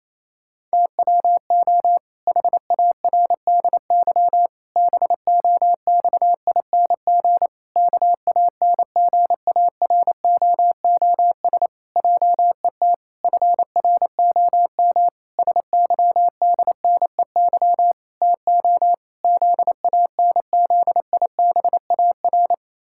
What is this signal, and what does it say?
Morse code, heard as a keyed tone, about 27 wpm, 700 Hz. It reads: TWO HARDY BOXING KANGAROOS JET FROM SYDNEY TO ZANZIBAR
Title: Pangram Challenge v2 - 28wpm